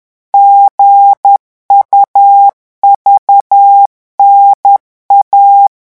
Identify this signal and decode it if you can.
Morse code, heard as a keyed tone, about 10 wpm, 800 Hz. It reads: GUVNA
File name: morse mp3 / revision3-vitesse 10 mots